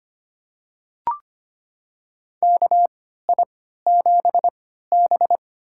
Morse code, heard as a keyed tone, about 25 wpm, spaced as for 14 wpm, 700 Hz. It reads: KI7B
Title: USA Call Signs - 4 Characters - 3x Character Spacing - 25wpm